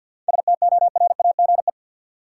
Morse code, heard as a keyed tone, about 49 wpm, 700 Hz. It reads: STORAGE
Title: Seven Letter Words - 50wpm